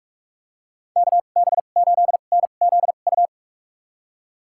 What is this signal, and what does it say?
Morse code, heard as a keyed tone, about 42 wpm, spaced as for 31 wpm, 700 Hz. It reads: KB8NZU
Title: USA Call Signs - 6 Characters - 2x Character Spacing - 45wpm